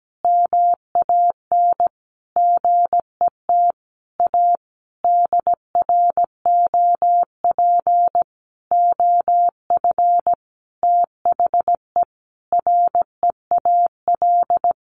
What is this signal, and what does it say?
Morse code, heard as a keyed tone, about 17 wpm, 700 Hz. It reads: MAN GET A DROP OF THE REAL
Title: Sentences from Top 1400 Words 17wpm